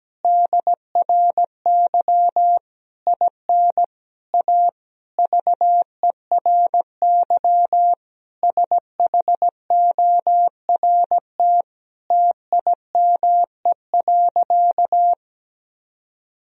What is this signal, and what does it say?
Morse code, heard as a keyed tone, about 17 wpm, 700 Hz. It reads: DRY IN A VERY SHORT TIME.
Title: Sentences from Top 1000 Words - Review Part 3 - 17wpm